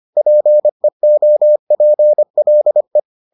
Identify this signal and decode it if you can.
Morse code, heard as a keyed tone, about 25 wpm, 600 Hz. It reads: PEOPLE